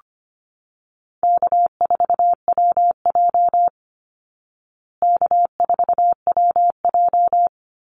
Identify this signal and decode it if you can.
Morse code, heard as a keyed tone, about 25 wpm, 700 Hz. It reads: K4WJ K4WJ
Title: Call Signs Repeated 2x - 25wpm